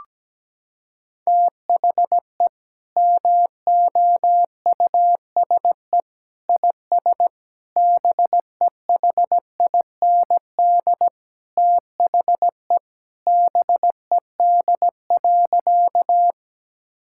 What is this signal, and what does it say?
Morse code, heard as a keyed tone, about 17 wpm, 700 Hz. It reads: THE MOUSE IS BEHIND THE BED.
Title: Sentences from Top 1000 Words - Review Part 5 - 17wpm